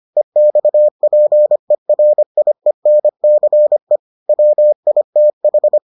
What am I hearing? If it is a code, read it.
Morse code, heard as a keyed tone, about 25 wpm, 600 Hz. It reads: EXPERIENCE WITH